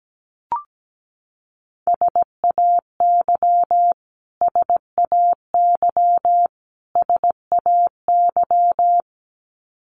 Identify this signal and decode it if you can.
Morse code, heard as a keyed tone, about 17 wpm, 700 Hz. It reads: SAY SAY SAY
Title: Top 500 Words Repeated 17wpm